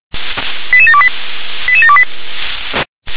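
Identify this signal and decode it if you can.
Five-tone selective call (ZVEI), heard as quick runs of five tones, tones of about 70 ms. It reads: 88728 88728
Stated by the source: ZVEI five-tone call received by SDR.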